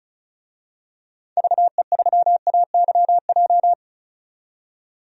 Morse code, heard as a keyed tone, about 34 wpm, 700 Hz. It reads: VE3AYJ
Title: DXCC-100 Call Signs - 6 Characters - 35wpm